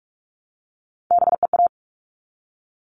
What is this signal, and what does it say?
Morse code, heard as a keyed tone, about 44 wpm, 700 Hz. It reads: BEA
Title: All Cut Numbers - 3-Digit 45wpm